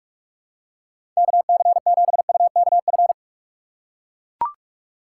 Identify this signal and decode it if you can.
Morse code, heard as a keyed tone, about 44 wpm, 700 Hz. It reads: KC7UKF
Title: USA Call Signs - 6 Characters - 4x Character Spacing - 45wpm